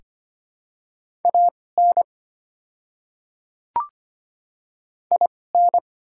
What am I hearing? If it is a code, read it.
Morse code, heard as a keyed tone, about 25 wpm, spaced as for 18 wpm, 700 Hz. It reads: AN IN
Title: 2x Character Spacing - Top 200 Words - 25wpm